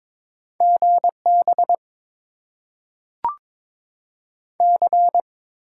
Morse code, heard as a keyed tone, about 22 wpm, 700 Hz. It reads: GB C
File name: Top 90 QSO Elements - 8x Character Spacing - 22wpm